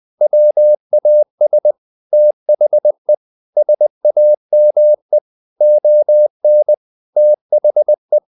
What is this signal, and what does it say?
Morse code, heard as a keyed tone, about 20 wpm, 600 Hz. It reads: WAS THE SAME ON THE
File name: the old man and the sea part 2 20wpm